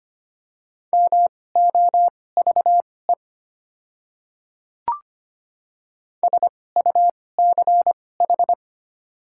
Morse code, heard as a keyed tone, about 25 wpm, spaced as for 18 wpm, 700 Hz. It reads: MOVE SUCH
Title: 2x Character Spacing - Top 200 Words - 25wpm